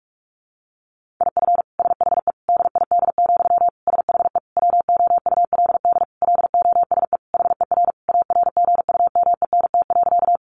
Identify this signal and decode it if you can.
Morse code, heard as a keyed tone, about 44 wpm, 700 Hz. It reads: IF SHE DID, SHE WOULD LOSE HER ARGUMENT.
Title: Sentences from Top 1400 Words 45wpm